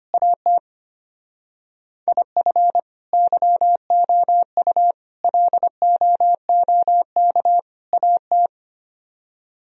Morse code, heard as a keyed tone, about 32 wpm, 700 Hz. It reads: AT IF YOU LOOK AT